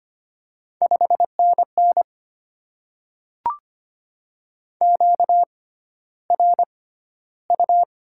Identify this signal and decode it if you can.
Morse code, heard as a keyed tone, about 25 wpm, spaced as for 13 wpm, 700 Hz. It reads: 5NN QRU
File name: Top 90 QSO Elements - 6x Character Spacing - 25wpm